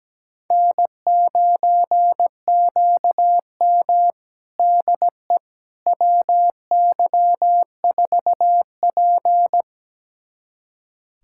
Morse code, heard as a keyed tone, about 17 wpm, 700 Hz. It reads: N9QM DE WY4P
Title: USA Callsign DE Callsign - 4 Characters - 17wpm